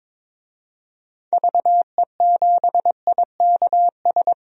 Morse code, heard as a keyed tone, about 22 wpm, 700 Hz. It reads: VE7IKS
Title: DXCC-100 Call Signs - 6 Characters - 22wpm